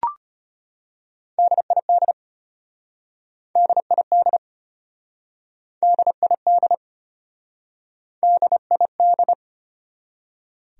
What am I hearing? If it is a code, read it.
Morse code, heard as a keyed tone, about 34 wpm, 700 Hz. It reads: DID DID DID DID